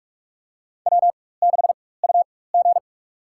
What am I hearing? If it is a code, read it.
Morse code, heard as a keyed tone, about 46 wpm, spaced as for 21 wpm, 700 Hz. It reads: W6UG
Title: USA Call Signs - 4x Character Spacing - 45wpm